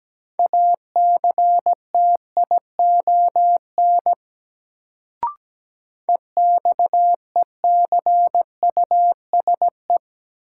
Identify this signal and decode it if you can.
Morse code, heard as a keyed tone, about 17 wpm, 700 Hz. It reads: ACTION EXECUSE
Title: Sentences from Top 700 Words - New Word Drill - 17wpm